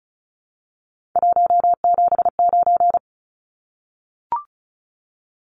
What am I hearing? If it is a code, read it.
Morse code, heard as a keyed tone, about 34 wpm, 700 Hz. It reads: 179
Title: Numbers - 3-Digit 35wpm